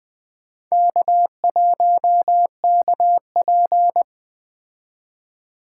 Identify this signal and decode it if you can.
Morse code, heard as a keyed tone, about 20 wpm, 700 Hz. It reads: K1KP
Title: Speed Racing - USA Call Signs 17wpm